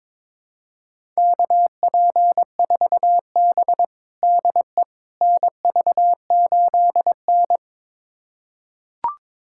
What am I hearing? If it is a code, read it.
Morse code, heard as a keyed tone, about 22 wpm, 700 Hz. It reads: KP4B DE NV8N
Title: USA Callsign DE Callsign - 4 Characters - 22wpm